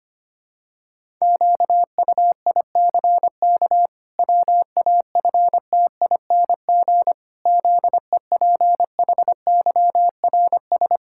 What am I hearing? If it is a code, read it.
Morse code, heard as a keyed tone, about 25 wpm, 700 Hz. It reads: QUICK WAFTING ZEPHYRS
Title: Pangram Challenge v2 - 25wpm